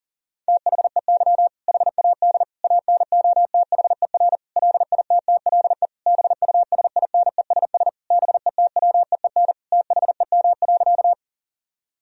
Morse code, heard as a keyed tone, about 40 wpm, 700 Hz. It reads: THEY HAD ANOTHER LITTLE BUSINESS BETWEEN THEM.
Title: Sentences from Top 1000 Words - Review Part 1 - 40wpm